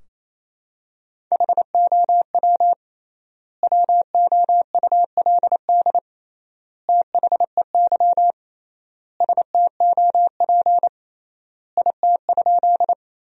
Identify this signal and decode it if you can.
Morse code, heard as a keyed tone, about 28 wpm, 700 Hz. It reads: HOW WOULD THEY STOP IT?
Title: 3x Word-Spacing - Sentences from Top 300 Words - 28wpm